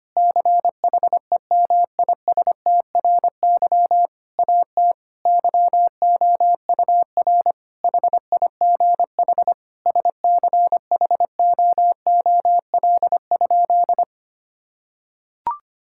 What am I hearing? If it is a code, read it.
Morse code, heard as a keyed tone, about 25 wpm, 700 Hz. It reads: CHEMISTRY AT YOUR HIGH SCHOOL?
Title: Sentences from Top 1000 Words - Review Part 6 - 25wpm